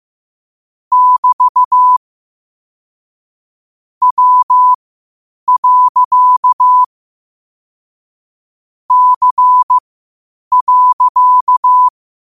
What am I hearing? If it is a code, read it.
Morse code, heard as a keyed tone, about 15 wpm, spaced as for 8 wpm, 1000 Hz. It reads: <BT> W. C.
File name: qotdfiles / 8wpm